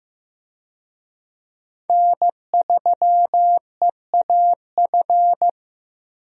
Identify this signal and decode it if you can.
Morse code, heard as a keyed tone, about 16 wpm, 700 Hz. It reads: N3EAF
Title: USA Call Signs - 5 Characters - 15wpm